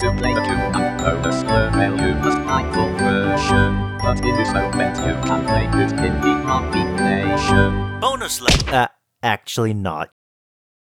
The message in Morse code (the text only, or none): none